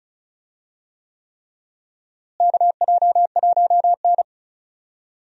Morse code, heard as a keyed tone, about 34 wpm, 700 Hz. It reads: KJ1N
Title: USA Call Signs - 4 Characters - Repeated 3x - 35wpm